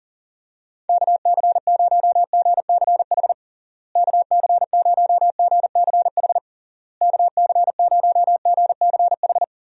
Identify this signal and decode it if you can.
Morse code, heard as a keyed tone, about 37 wpm, 700 Hz. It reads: KC0GCH KC0GCH KC0GCH
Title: USA Call Signs - 6 Characters - Repeated 3x - 40wpm